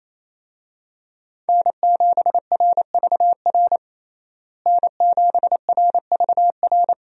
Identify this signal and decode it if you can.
Morse code, heard as a keyed tone, about 28 wpm, 700 Hz. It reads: N7RVR N7RVR
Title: USA Call Signs - 5 Characters - Repeated 2x - 28wpm